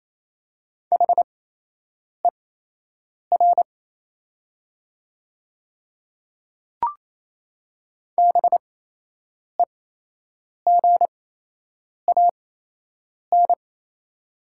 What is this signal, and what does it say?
Morse code, heard as a keyed tone, about 28 wpm, spaced as for 8 wpm, 700 Hz. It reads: HER BEGAN
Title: 8x Character Spacing - Top 500 Words - 28wpm